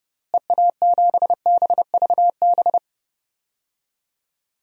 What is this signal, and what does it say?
Morse code, heard as a keyed tone, about 30 wpm, 700 Hz. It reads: EA7BVB
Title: DXCC-100 Call Signs - 6 Characters - Repeated 2x - 30wpm